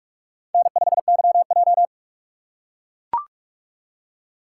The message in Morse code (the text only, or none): N5YJ